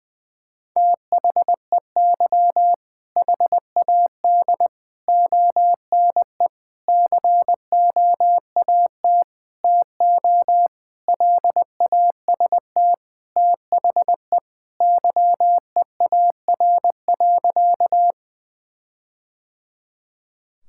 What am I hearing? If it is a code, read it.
Morse code, heard as a keyed tone, about 20 wpm, 700 Hz. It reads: THEY HAD ONE COAT TO LAST THE YEAR.
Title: Sentences from Top 1600 Words 20wpm